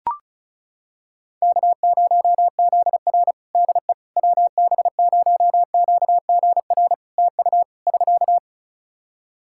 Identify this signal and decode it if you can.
Morse code, heard as a keyed tone, about 36 wpm, 700 Hz. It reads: K0ZR DE WB0QGR TU <SK>
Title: Practice QSO - Exchange 4 - Ending - 35wpm